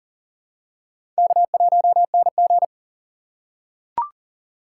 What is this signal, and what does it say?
Morse code, heard as a keyed tone, about 40 wpm, 700 Hz. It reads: K1NG
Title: USA Call Signs - 6x Character Spacing - 40wpm